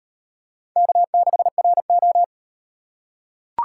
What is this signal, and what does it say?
Morse code, heard as a keyed tone, about 35 wpm, 700 Hz. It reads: K6RO